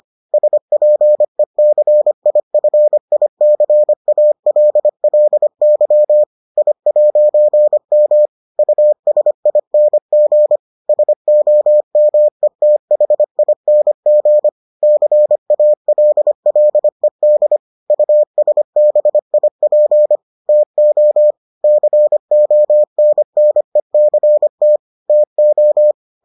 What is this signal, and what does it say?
Morse code, heard as a keyed tone, about 25 wpm, 600 Hz. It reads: SPECIFICALLY I'M USING SOMETHING CALLED USBIP TO CONNECT TO